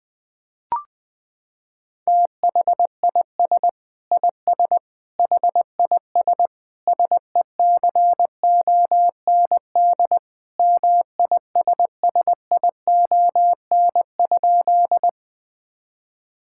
Morse code, heard as a keyed tone, about 20 wpm, 700 Hz. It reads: THIS IS HIS SECOND MISSION?
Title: Sentences from Top 1600 Words 20wpm